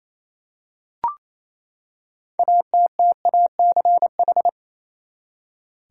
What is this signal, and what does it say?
Morse code, heard as a keyed tone, about 28 wpm, 700 Hz. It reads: ATTACH
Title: Six Letter Words - 28wpm